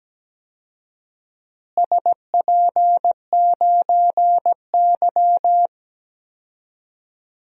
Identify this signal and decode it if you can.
Morse code, heard as a keyed tone, about 17 wpm, 700 Hz. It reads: SP9Y
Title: DXCC-100 Call Signs - 4 Characters - Repeated 2x - 17wpm